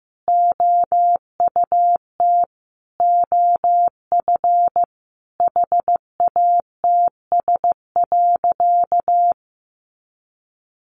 Morse code, heard as a keyed tone, about 15 wpm, 700 Hz. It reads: OUT OF HATS.
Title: Easy Phrases 15wpm